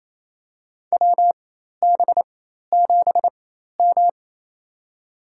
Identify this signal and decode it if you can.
Morse code, heard as a keyed tone, about 28 wpm, spaced as for 13 wpm, 700 Hz. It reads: WB7M
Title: USA Call Signs - 4 Characters - 4x Character Spacing - 28wpm